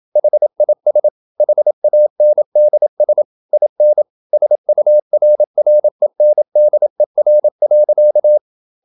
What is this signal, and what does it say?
Morse code, heard as a keyed tone, about 27 wpm, 600 Hz. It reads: HIS HANDS IN SURRENDER.
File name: Never 0027wpm